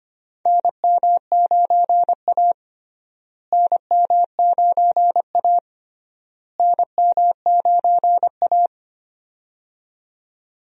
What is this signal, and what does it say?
Morse code, heard as a keyed tone, about 25 wpm, 700 Hz. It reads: NM9A NM9A NM9A
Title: USA Call Signs - 4 Characters - Repeated 3x - 25wpm